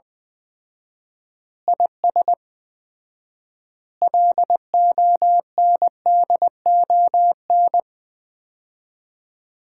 Morse code, heard as a keyed tone, about 20 wpm, 700 Hz. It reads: IS LONDON
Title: Common QSO Phrases - 4x Word-Spacing - 20wpm